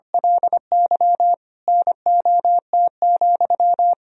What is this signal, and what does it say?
Morse code, heard as a keyed tone, about 25 wpm, 700 Hz. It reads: LY NOT,
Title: Sentences from Top 1000 Words - Review Part 1 - 25wpm